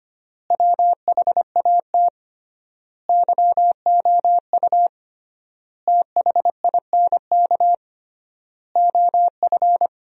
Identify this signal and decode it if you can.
Morse code, heard as a keyed tone, about 25 wpm, 700 Hz. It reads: WHAT YOU THINK OF
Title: 3x Word-Spacing - Sentences from Top 200 Words - 25wpm